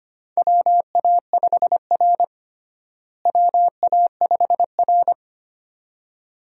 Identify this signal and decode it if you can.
Morse code, heard as a keyed tone, about 25 wpm, 700 Hz. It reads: WA5R WA5R
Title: USA Call Signs - 4 Characters - Repeated 2x - 25wpm